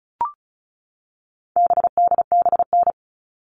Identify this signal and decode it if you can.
Morse code, heard as a keyed tone, about 34 wpm, 700 Hz. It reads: BDBN
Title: All Cut Numbers - 4-Digit 35wpm